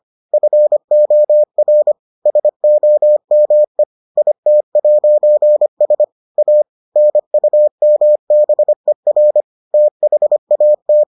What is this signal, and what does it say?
Morse code, heard as a keyed tone, about 25 wpm, 600 Hz. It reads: FOR SOME IT'S A NUMBER THAT